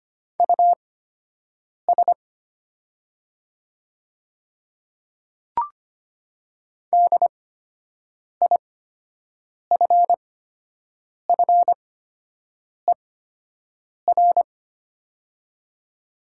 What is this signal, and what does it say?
Morse code, heard as a keyed tone, about 25 wpm, spaced as for 7 wpm, 700 Hz. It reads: US DIFFER
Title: 8x Character Spacing - Top 300 Words - 25wpm